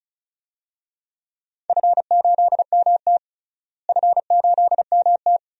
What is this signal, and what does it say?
Morse code, heard as a keyed tone, about 34 wpm, 700 Hz. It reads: F8MT F8MT
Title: DXCC-100 Call Signs - 4 Characters - Repeated 2x - 35wpm